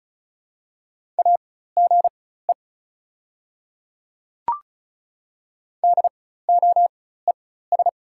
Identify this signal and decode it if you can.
Morse code, heard as a keyed tone, about 36 wpm, spaced as for 16 wpm, 690 Hz. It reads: AGE DOES
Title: 4x Character Spacing - Top 500 Words - 35wpm